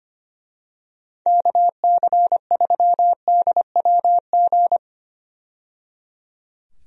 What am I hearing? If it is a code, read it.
Morse code, heard as a keyed tone, about 25 wpm, 700 Hz. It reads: KC3DWG
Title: USA Call Signs - 6 Characters - 25wpm